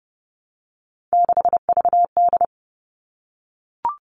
Morse code, heard as a keyed tone, about 30 wpm, 700 Hz. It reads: 6VD
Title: All Cut Numbers - 3-Digit 30wpm